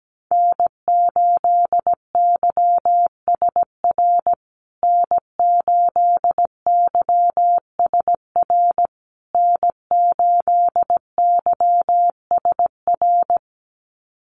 Morse code, heard as a keyed tone, about 17 wpm, 700 Hz. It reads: N8YSR N8YSR N8YSR
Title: USA Call Signs Repeated 3x 17wpm